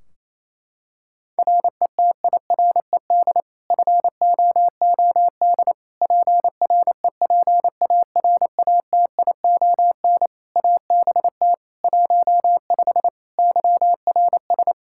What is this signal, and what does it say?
Morse code, heard as a keyed tone, about 27 wpm, 700 Hz. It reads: RETIRED FOOD PREPARATION ABT 15 YRS
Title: K1USN SST QSO Practice 28wpm